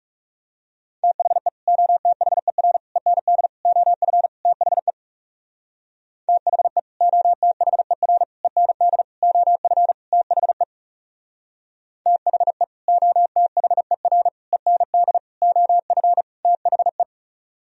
Morse code, heard as a keyed tone, about 38 wpm, 700 Hz. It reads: THE OTHER END OF THE THE OTHER END OF THE THE OTHER END OF THE